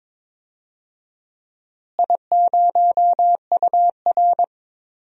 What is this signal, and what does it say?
Morse code, heard as a keyed tone, about 22 wpm, 700 Hz. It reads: I0UR
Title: DXCC-100 Call Signs - 4 Characters - Repeated 2x - 22wpm